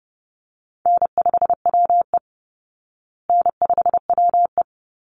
Morse code, heard as a keyed tone, about 30 wpm, 700 Hz. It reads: N5WE N5WE
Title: Easy Call Signs Repeated 2x - 30wpm